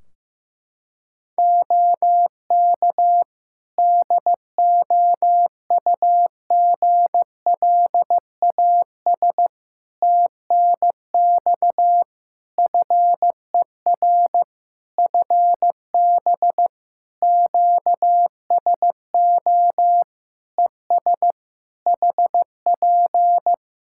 Morse code, heard as a keyed tone, about 15 wpm, 700 Hz. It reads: OK DOUGLAS TNX FER FB QSO ES HP
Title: Practice QSO - Exchange 4 - Ending - 15wpm